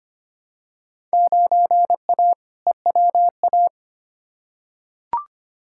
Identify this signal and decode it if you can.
Morse code, heard as a keyed tone, about 25 wpm, 700 Hz. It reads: 9A EWA
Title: Field Day Exchanges Encore - 25wpm